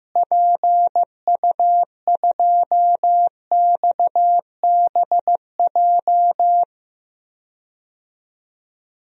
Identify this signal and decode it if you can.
Morse code, heard as a keyed tone, about 15 wpm, 700 Hz. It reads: PU2XBJ
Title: DXCC-100 Call Signs - 6 Characters - Repeated 2x - 15wpm